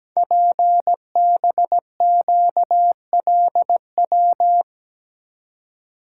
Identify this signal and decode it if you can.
Morse code, heard as a keyed tone, about 17 wpm, 700 Hz. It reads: PBQLW
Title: Code Groups - 5 Letters - 17wpm